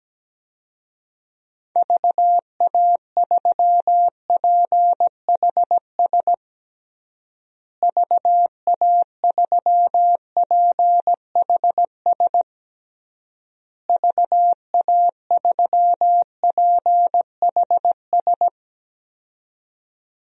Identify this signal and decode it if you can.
Morse code, heard as a keyed tone, about 17 wpm, 700 Hz. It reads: VA3PHS VA3PHS VA3PHS
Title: DXCC-100 Call Signs - 6 Characters - Repeated 3x - 17wpm